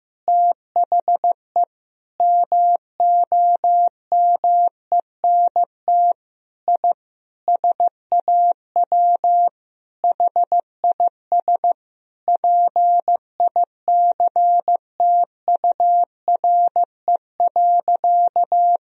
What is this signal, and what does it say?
Morse code, heard as a keyed tone, about 16 wpm, 700 Hz. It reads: THE MOMENT I SAW HIS PICTURE.
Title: Sentences from Top 1000 Words - Review Part 1 - 15wpm